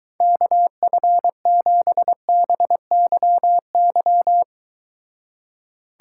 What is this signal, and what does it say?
Morse code, heard as a keyed tone, about 23 wpm, 700 Hz. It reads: KF7BYY